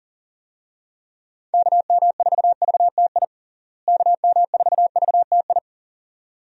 Morse code, heard as a keyed tone, about 40 wpm, 700 Hz. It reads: KM4VTI KM4VTI
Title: USA Call Signs - 6 Characters - Repeated 2x - 40wpm